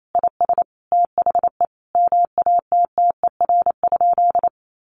Morse code, heard as a keyed tone, about 28 wpm, 700 Hz. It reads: IS THE MATTER?
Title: Sentences from Top 1100 Words 28wpm